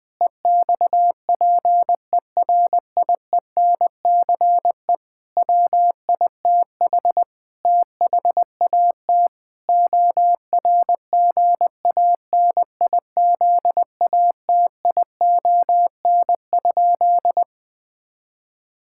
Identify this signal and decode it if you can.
Morse code, heard as a keyed tone, about 20 wpm, 700 Hz. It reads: EXPERIENCE WITH THAT ORGANIZATION?
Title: Sentences from Top 1000 Words - Review Part 5 - 20wpm